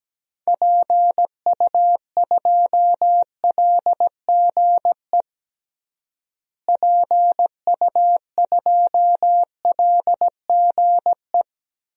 Morse code, heard as a keyed tone, about 17 wpm, 700 Hz. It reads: PU2LGE PU2LGE
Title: DXCC-100 Call Signs - 6 Characters - Repeated 2x - 17wpm